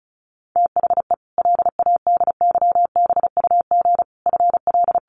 Morse code, heard as a keyed tone, about 34 wpm, 700 Hz. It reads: THE LADYBUG FL
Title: Additional Easy Phrases 35wpm